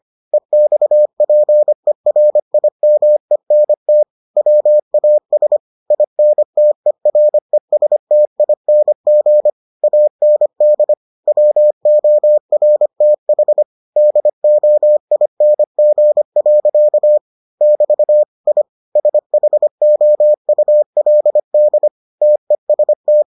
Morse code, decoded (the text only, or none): EXPERIMENT WAS INTERESTING AND WORTH DOING. <BT> I SHOULD TEST